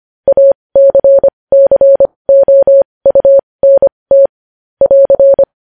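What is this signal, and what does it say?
Morse code, heard as a keyed tone, about 25 wpm, spaced as for 20 wpm, 550 Hz. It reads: ACCOUNT <AR>
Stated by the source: Morse code at 20 wpm.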